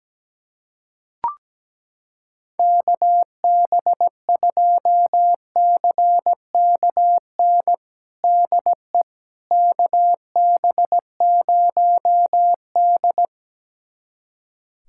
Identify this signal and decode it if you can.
Morse code, heard as a keyed tone, about 17 wpm, 700 Hz. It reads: KB2CKN DE KB0D
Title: USA Callsign DE Callsign 17wpm